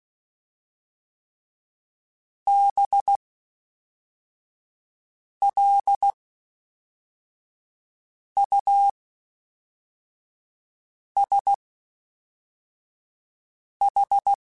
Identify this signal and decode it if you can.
Morse code, heard as a keyed tone, about 16 wpm, 800 Hz. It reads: B L U S H
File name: Special INGs / Special INGs-mp3s / Fast ing @10